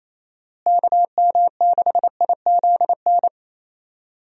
Morse code, heard as a keyed tone, about 28 wpm, 700 Hz. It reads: KM6IZN
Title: USA Call Signs - 6 Characters - Repeated 3x - 28wpm